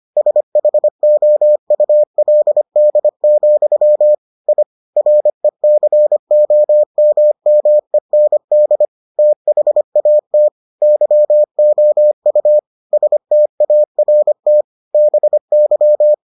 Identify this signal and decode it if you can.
Morse code, heard as a keyed tone, about 25 wpm, 600 Hz. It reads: SHOULD, I RECOMMEND THAT YOU START BY